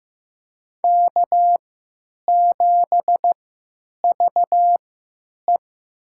Morse code, heard as a keyed tone, about 15 wpm, spaced as for 9 wpm, 700 Hz. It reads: K7VE
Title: USA Call Signs - 3x Character Spacing - 15wpm